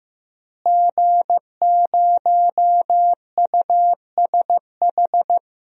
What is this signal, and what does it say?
Morse code, heard as a keyed tone, about 15 wpm, 700 Hz. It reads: G0USH
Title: DXCC-100 Call Signs - 5 Characters - Repeated 2x - 15wpm